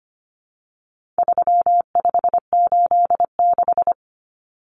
Morse code, heard as a keyed tone, about 25 wpm, 700 Hz. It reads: 3586
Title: Numbers - 4-Digit 25wpm